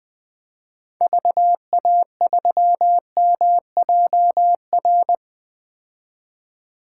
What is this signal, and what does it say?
Morse code, heard as a keyed tone, about 20 wpm, 700 Hz. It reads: VA3MJR
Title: 2023 POTA Hunter Call Signs - 20wpm